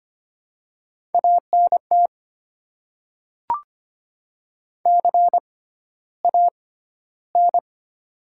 Morse code, heard as a keyed tone, about 25 wpm, spaced as for 13 wpm, 700 Hz. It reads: ANT CAN
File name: Top 90 QSO Elements - 6x Character Spacing - 25wpm